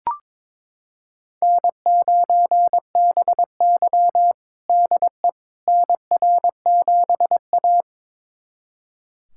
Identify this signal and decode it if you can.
Morse code, heard as a keyed tone, about 22 wpm, 700 Hz. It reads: N9BY DE NR7A
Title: USA Callsign DE Callsign - 4 Characters - 22wpm